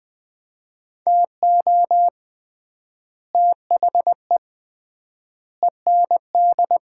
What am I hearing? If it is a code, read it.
Morse code, heard as a keyed tone, about 21 wpm, 700 Hz. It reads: TO THE END
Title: 3x Word-Spacing - Sets of 4 Words - 20wpm